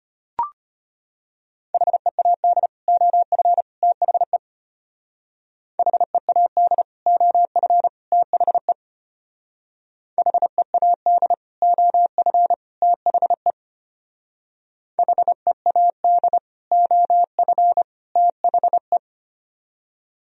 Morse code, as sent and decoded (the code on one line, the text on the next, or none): .... .- -.. / --- ..-. / - .... / .... .- -.. / --- ..-. / - .... / .... .- -.. / --- ..-. / - .... / .... .- -.. / --- ..-. / - .... .
HEAD OF THE HEAD OF THE HEAD OF THE HEAD OF THE